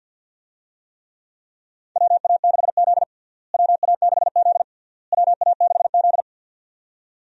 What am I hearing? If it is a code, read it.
Morse code, heard as a keyed tone, about 48 wpm, 700 Hz. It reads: WA6Z WA6Z WA6Z
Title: USA Call Signs - 4 Characters - Repeated 3x - 50wpm